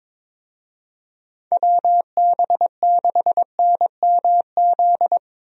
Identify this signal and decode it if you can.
Morse code, heard as a keyed tone, about 22 wpm, 700 Hz. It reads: WB6NMZ
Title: USA Call Signs - 6 Characters - 22wpm